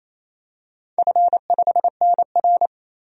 Morse code, heard as a keyed tone, about 28 wpm, 700 Hz. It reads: F5NR
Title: DXCC-100 Call Signs - 4 Characters - 28wpm